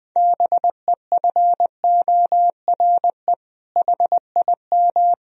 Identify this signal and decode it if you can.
Morse code, heard as a keyed tone, about 20 wpm, 700 Hz. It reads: BEFORE HIM
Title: Sentences from Top 1000 Words - Review Part 2 - 20wpm